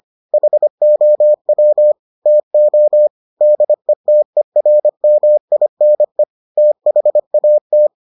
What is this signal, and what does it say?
Morse code, heard as a keyed tone, about 25 wpm, 600 Hz. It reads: HOW TO DETERMINE THAT